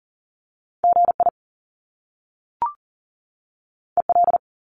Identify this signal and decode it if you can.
Morse code, heard as a keyed tone, about 39 wpm, 700 Hz. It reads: GI EL
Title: Top Two Letter Combinations in Words 40wpm